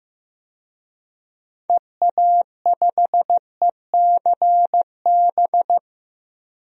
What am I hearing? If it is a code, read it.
Morse code, heard as a keyed tone, about 15 wpm, 700 Hz. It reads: EA5ECB
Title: DXCC-100 Call Signs - 6 Characters - Repeated 3x - 15wpm